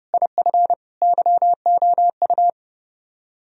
Morse code, heard as a keyed tone, about 35 wpm, 700 Hz. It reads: IF YOU
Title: Speed-Racing - Sets of 3 Words 30wpm